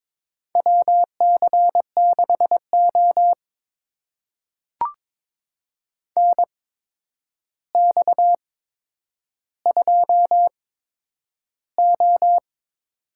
Morse code, heard as a keyed tone, about 22 wpm, spaced as for 9 wpm, 700 Hz. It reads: WC6O NX2O
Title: USA Call Signs - 4 Characters - 8x Character Spacing - 22wpm